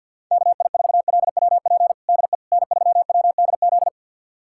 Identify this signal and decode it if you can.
Morse code, heard as a keyed tone, about 51 wpm, 700 Hz. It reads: KI4LWP DE N3WDZ
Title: USA Callsign DE Callsign 50wpm